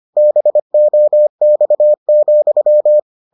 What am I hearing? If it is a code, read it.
Morse code, heard as a keyed tone, about 25 wpm, 600 Hz. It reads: BOX,